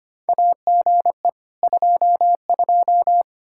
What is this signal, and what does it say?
Morse code, heard as a keyed tone, about 25 wpm, 700 Hz. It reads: AGE 22
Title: Practice QSO - Exchange 3 - More Info - 25wpm